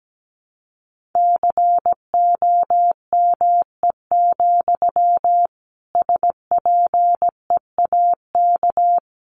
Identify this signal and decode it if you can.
Morse code, heard as a keyed tone, about 17 wpm, 700 Hz. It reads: COME, SPEAK
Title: Sentences from Top 1100 Words 17wpm